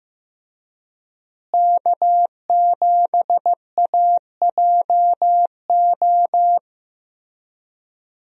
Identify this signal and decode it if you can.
Morse code, heard as a keyed tone, about 16 wpm, 700 Hz. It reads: K7AJO
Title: USA Call Signs - 5 Characters - Repeated 3x - 15wpm